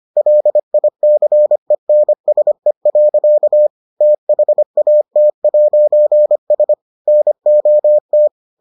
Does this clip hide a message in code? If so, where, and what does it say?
Yes: Morse code from 0.2 to 8.3 s, heard as a keyed tone, about 25 wpm, 600 Hz. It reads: LICENSE. THAT'S NOT